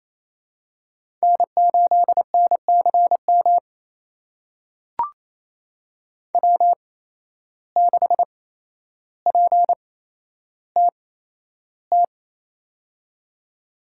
Morse code, heard as a keyed tone, about 28 wpm, spaced as for 12 wpm, 700 Hz. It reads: N8NCM W6PTT
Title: USA Call Signs - 5 Characters - 8x Character Spacing - 28wpm